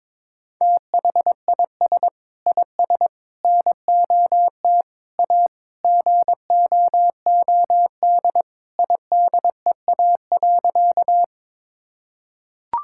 Morse code, heard as a keyed tone, about 22 wpm, 700 Hz. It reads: THIS IS NOT A GOOD IDEA.
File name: practice / 22wpm / Sentences with 4-Word Phrases - 22wpm